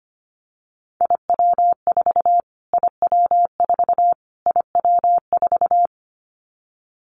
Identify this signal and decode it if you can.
Morse code, heard as a keyed tone, about 25 wpm, 700 Hz. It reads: IW4 IW4 IW4
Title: Callsign Prefixes for Top 25 DXCC - Repeated 3x - 25wpm